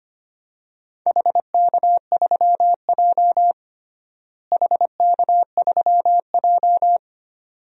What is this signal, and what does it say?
Morse code, heard as a keyed tone, about 25 wpm, 700 Hz. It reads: HK3J HK3J
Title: DXCC-100 Call Signs - 4 Characters - Repeated 2x - 25wpm